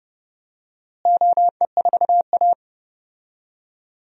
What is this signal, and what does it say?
Morse code, heard as a keyed tone, about 31 wpm, 700 Hz. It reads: OE4A
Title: DXCC-100 Call Signs - 4 Characters - Repeated 2x - 30wpm